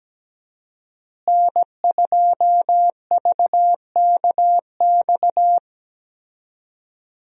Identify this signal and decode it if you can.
Morse code, heard as a keyed tone, about 17 wpm, 700 Hz. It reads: N2VKX
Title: USA Call Signs - 5 Characters - Repeated 2x - 17wpm